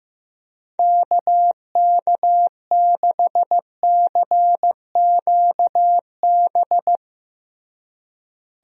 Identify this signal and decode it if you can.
Morse code, heard as a keyed tone, about 15 wpm, 700 Hz. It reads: KK6CQB
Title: USA Call Signs - 6 Characters - Repeated 2x - 15wpm